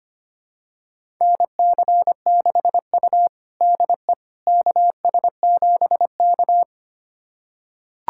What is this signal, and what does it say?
Morse code, heard as a keyed tone, about 25 wpm, 700 Hz. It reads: NC6U DE KS7K
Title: USA Callsign DE Callsign - 4 Characters - 25wpm